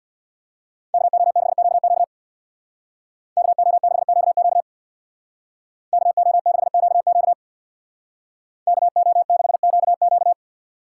Morse code, heard as a keyed tone, about 63 wpm, 700 Hz. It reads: KO6QQ KO6QQ KO6QQ KO6QQ